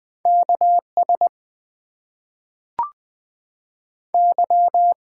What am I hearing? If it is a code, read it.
Morse code, heard as a keyed tone, about 21 wpm, 700 Hz. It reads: KS Y
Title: US States and Canadian Provinces 20wpm